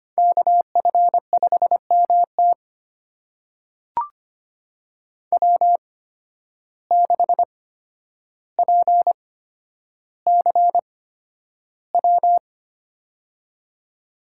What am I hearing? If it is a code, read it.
Morse code, heard as a keyed tone, about 25 wpm, spaced as for 11 wpm, 700 Hz. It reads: KF5MT W6PCW
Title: USA Call Signs - 5 Characters - 8x Character Spacing - 25wpm